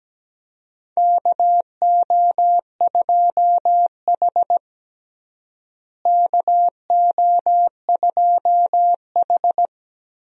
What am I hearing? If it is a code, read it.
Morse code, heard as a keyed tone, about 17 wpm, 700 Hz. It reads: KO2H KO2H
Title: USA Call Signs - 4 Characters - Repeated 3x - 17wpm